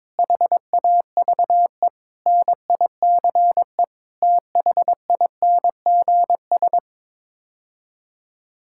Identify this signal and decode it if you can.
Morse code, heard as a keyed tone, about 22 wpm, 700 Hz. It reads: HAVE NICE THINGS
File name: practice / 22wpm / Sets of 8 Words - 22wpm